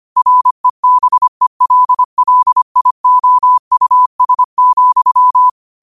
Morse code, heard as a keyed tone, about 25 wpm, 1000 Hz. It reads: REBELLIOUS,
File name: qotdfiles / 25wpm